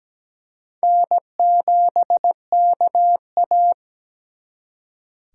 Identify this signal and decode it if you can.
Morse code, heard as a keyed tone, about 21 wpm, 700 Hz. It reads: N7KA
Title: Speed Racing - USA 1x2 Call Signs 17wpm